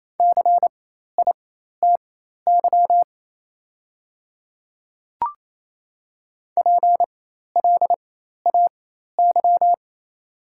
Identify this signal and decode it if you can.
Morse code, heard as a keyed tone, about 28 wpm, spaced as for 13 wpm, 700 Hz. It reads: CITY PLAY
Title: 4x Character Spacing - Top 300 Words - 28wpm